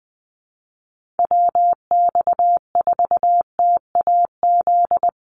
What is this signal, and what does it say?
Morse code, heard as a keyed tone, about 20 wpm, 700 Hz. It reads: WX4TAZ
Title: USA Call Signs Repeated 2x 20wpm